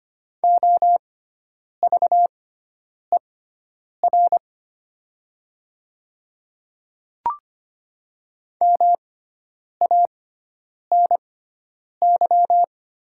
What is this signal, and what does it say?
Morse code, heard as a keyed tone, about 25 wpm, spaced as for 9 wpm, 700 Hz. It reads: OVER MANY
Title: 6x Character Spacing - Top 100 Words - 25wpm